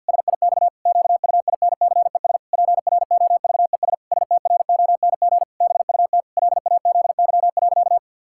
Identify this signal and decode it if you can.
Morse code, heard as a keyed tone, about 50 wpm, 700 Hz. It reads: SIX QUINCES PROVES STRONG BUT LAZY.
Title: Pangram Challenge v2 - 50wpm